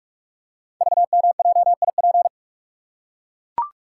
Morse code, heard as a keyed tone, about 44 wpm, 700 Hz. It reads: VMJIP
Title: Code Groups - 5 Letter:Numbers - 45wpm